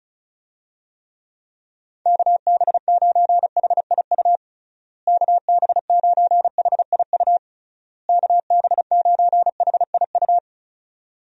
Morse code, heard as a keyed tone, about 36 wpm, 700 Hz. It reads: KB9HIU KB9HIU KB9HIU
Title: USA Call Signs - 6 Characters - Repeated 3x - 35wpm